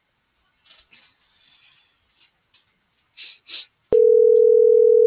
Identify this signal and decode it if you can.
Ringback tone (North America), 440 + 480 Hz.